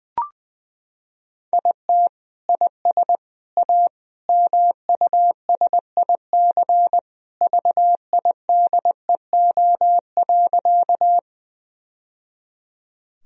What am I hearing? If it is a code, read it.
Morse code, heard as a keyed tone, about 20 wpm, 700 Hz. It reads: IT IS A MUSIC VIDEO.
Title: Sentences from Top 1000 Words - Review Part 4 - 20wpm